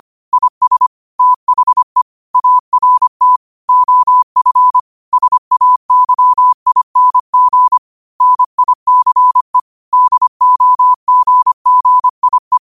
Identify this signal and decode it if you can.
Morse code, heard as a keyed tone, about 25 wpm, 1000 Hz. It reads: IS THE ART OF SAYING NICE DOGGIE
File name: qotdfiles / 25wpm